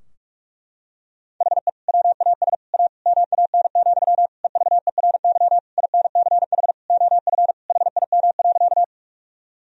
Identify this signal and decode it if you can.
Morse code, heard as a keyed tone, about 46 wpm, 700 Hz. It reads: HE WAS A MAN, EVERY INCH OF HIM.
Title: Sentences from Top 1000 Words - Review Part 3 - 45wpm